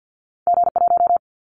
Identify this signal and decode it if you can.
Morse code, heard as a keyed tone, about 50 wpm, 700 Hz. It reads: G1
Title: Callsign Prefixes for Top 25 DXCC - Repeated 3x - 50wpm